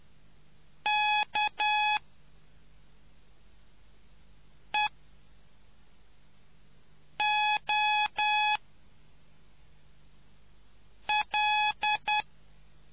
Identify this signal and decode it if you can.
Morse code, heard as a keyed tone, about 10 wpm, spaced as for 3 wpm, 840 Hz. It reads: K EO L